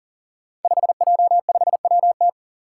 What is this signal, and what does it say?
Morse code, heard as a keyed tone, about 40 wpm, 700 Hz. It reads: 5J5WT
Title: Five Challenging Letter-Number Combinations 40wpm